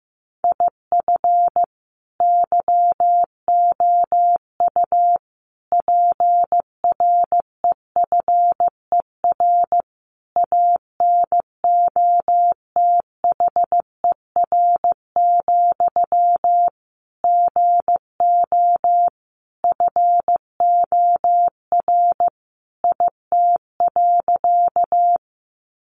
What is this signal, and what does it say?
Morse code, heard as a keyed tone, about 15 wpm, 700 Hz. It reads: IF YOU PREFER ANOTHER, GO FOR IT.
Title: Sentences from Top 1100 Words 15wpm